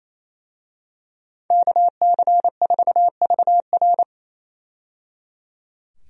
Morse code, heard as a keyed tone, about 28 wpm, 700 Hz. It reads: KC4VR
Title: USA Call Signs - 5 Characters - 28wpm